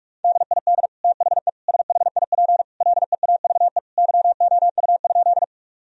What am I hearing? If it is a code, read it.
Morse code, heard as a keyed tone, about 46 wpm, 700 Hz. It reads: DID THE SHIP LEAVE YOU?
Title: Sentences from Top 1000 Words - Review Part 2 - 45wpm